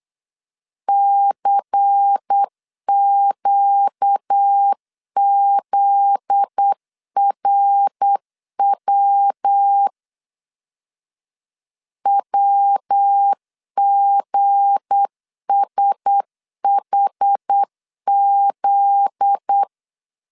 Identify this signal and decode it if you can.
Morse code, heard as a keyed tone, about 8 wpm, 800 Hz. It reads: CQZRW WGSHZ
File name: LEZIONI / 7 GF9Z / 7 GF9Z 8wpm 3213